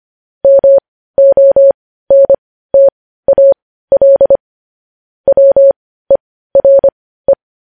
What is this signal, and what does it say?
Morse code, heard as a keyed tone, about 25 wpm, spaced as for 15 wpm, 550 Hz. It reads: MONTAL WERE
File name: podcast / News headlines-1x-25x15